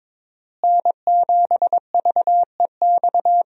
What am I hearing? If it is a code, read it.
Morse code, heard as a keyed tone, about 22 wpm, 700 Hz. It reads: N7VEX
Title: USA Call Signs - 5 Characters - Repeated 2x - 22wpm